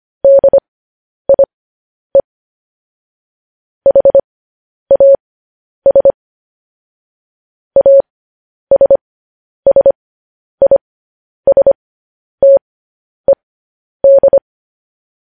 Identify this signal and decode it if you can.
Morse code, heard as a keyed tone, about 25 wpm, spaced as for 10 wpm, 550 Hz. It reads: DIE HAS ASSISTED
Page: news headlines in morse code at 10 wpm